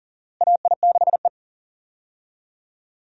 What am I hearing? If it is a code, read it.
Morse code, heard as a keyed tone, about 39 wpm, 700 Hz. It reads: AI6E
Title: USA Call Signs - 4 Characters - Repeated 3x - 40wpm